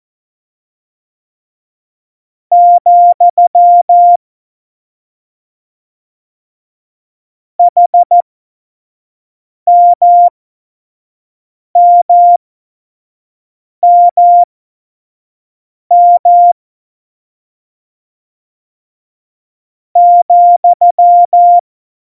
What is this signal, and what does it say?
Morse code, heard as a keyed tone, about 14 wpm, spaced as for 5 wpm, 700 Hz. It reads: , HMMMM ,
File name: Morsekurs / Witze / Witz 16. Oktober 2018, 25 F70